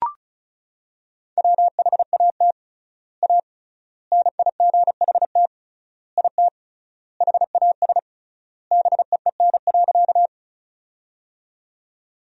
Morse code, heard as a keyed tone, about 36 wpm, 700 Hz. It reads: WHAT A NIGHT IT HAS BEEN.
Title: 3x Word-Spacing - Sentences from Top 300 Words - 35wpm